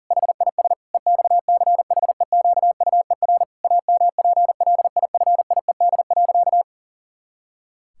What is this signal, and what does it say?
Morse code, heard as a keyed tone, about 40 wpm, 700 Hz. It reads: HIS EXCHEQUER AMPLIFIED.
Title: Pangram Challenge v2 - 40wpm